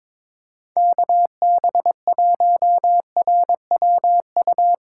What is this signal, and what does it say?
Morse code, heard as a keyed tone, about 22 wpm, 700 Hz. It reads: KB1RWU
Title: USA Call Signs - 4x Character Spacing - 22wpm